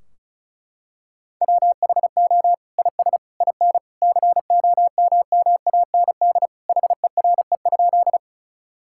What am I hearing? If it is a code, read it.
Morse code, heard as a keyed tone, about 36 wpm, 700 Hz. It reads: WHO IS IN COMMAND HERE?
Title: Sentences from Top 1600 Words 35wpm